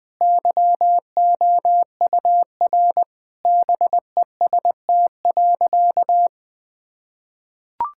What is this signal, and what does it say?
Morse code, heard as a keyed tone, about 20 wpm, 700 Hz. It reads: YOUR BEST.
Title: Sentences with 7-Word Phrases - Spoken Beginning - 20wpm